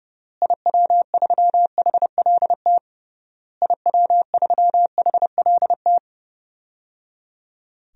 Morse code, heard as a keyed tone, about 30 wpm, 700 Hz. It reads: IW3HLT IW3HLT
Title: DXCC-100 Call Signs - 6 Characters - Repeated 3x - 30wpm